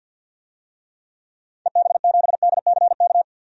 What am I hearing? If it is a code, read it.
Morse code, heard as a keyed tone, about 50 wpm, 700 Hz. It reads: EB7DCK
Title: DXCC-100 Call Signs - 6 Characters - 50wpm